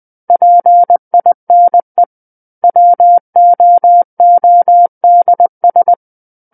Morse code, heard as a keyed tone, about 20 wpm, 700 Hz. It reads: PINE WOODS
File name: 20 WPM